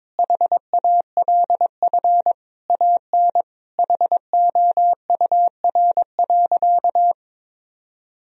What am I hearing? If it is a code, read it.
Morse code, heard as a keyed tone, about 21 wpm, 700 Hz. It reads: HALF AN HOUR.
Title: Sentences from Top 1000 Words - Review Part 1 - 22wpm